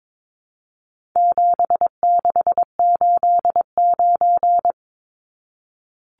Morse code, heard as a keyed tone, about 22 wpm, 700 Hz. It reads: 7689
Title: Numbers - 4-Digit 22wpm